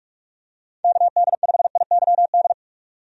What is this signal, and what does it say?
Morse code, heard as a keyed tone, about 43 wpm, 700 Hz. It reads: KD5IYD
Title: USA Call Signs - 6 Characters - 45wpm